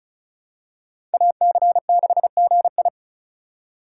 Morse code, heard as a keyed tone, about 36 wpm, 700 Hz. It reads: AC6GI
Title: USA Call Signs - 5 Characters - 4x Character Spacing - 35wpm